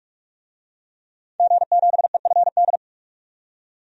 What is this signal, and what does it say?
Morse code, heard as a keyed tone, about 46 wpm, 700 Hz. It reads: G7EFD
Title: DXCC-100 Call Signs - 5 Characters - 45wpm